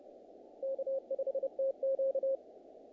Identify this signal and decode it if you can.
Morse code, heard as a keyed tone, about 30 wpm, 500 Hz. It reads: K5TQ
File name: contests / naqp-cw-k5tr-op-k5pi- -1800